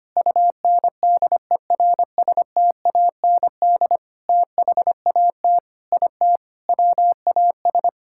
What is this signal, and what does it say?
Morse code, heard as a keyed tone, about 25 wpm, 700 Hz. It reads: UNDERSTAND THAT IT WAS
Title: Sentences from Top 1000 Words - Review Part 5 - 25wpm